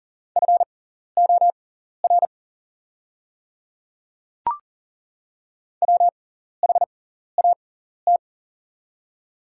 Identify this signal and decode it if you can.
Morse code, heard as a keyed tone, about 40 wpm, spaced as for 14 wpm, 700 Hz. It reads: FOR WHAT
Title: 6x Character Spacing - Top 200 Words - 40wpm